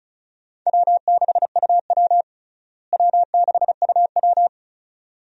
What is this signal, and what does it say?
Morse code, heard as a keyed tone, about 36 wpm, 700 Hz. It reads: W6UW W6UW
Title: USA Call Signs - 4 Characters - Repeated 2x - 35wpm